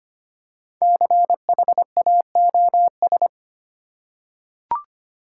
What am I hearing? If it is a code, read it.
Morse code, heard as a keyed tone, about 25 wpm, 700 Hz. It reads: CHAOS
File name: 5 Letter Words - First 3 Letters Given - 25wpm